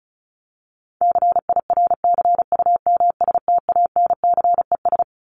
Morse code, heard as a keyed tone, about 34 wpm, 700 Hz. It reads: CIRCUMSTANCES
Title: ES:S Suffix - Top 100 Words with Base Words 35wpm